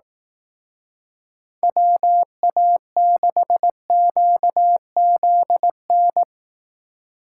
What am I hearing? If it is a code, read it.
Morse code, heard as a keyed tone, about 18 wpm, 700 Hz. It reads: WA6QZN
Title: Speed Racing - USA 2x3 Call Signs 15wpm